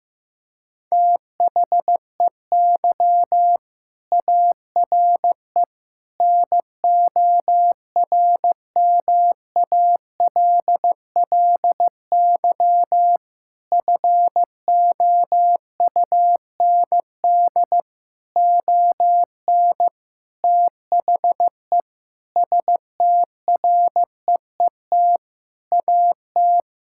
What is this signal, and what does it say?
Morse code, heard as a keyed tone, about 15 wpm, 700 Hz. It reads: THEY ARE NORMALLY FOUND ON THE STREET AT
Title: Sentences from Top 1000 Words - Review Part 6 - 15wpm